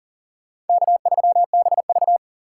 Morse code, heard as a keyed tone, about 51 wpm, 700 Hz. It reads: K3BV